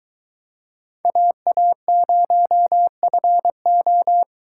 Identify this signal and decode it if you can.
Morse code, heard as a keyed tone, about 22 wpm, 700 Hz. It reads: AA0FO